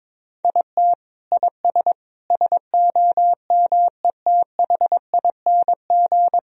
Morse code, heard as a keyed tone, about 22 wpm, 700 Hz. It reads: IT IS SOMETHING
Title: Sentences from Top 1000 Words - Review Part 3 - 22wpm